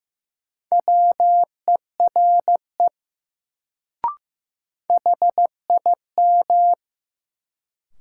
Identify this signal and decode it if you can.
Morse code, heard as a keyed tone, about 15 wpm, 700 Hz. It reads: WERE HIM
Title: Sets of 2 Words - New Word Drill - 15wpm